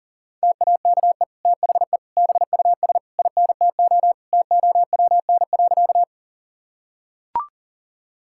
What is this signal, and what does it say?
Morse code, heard as a keyed tone, about 40 wpm, 700 Hz. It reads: TAKE THE BUS INTO TOWN.
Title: Sentences from Top 1000 Words - Review Part 6 - 40wpm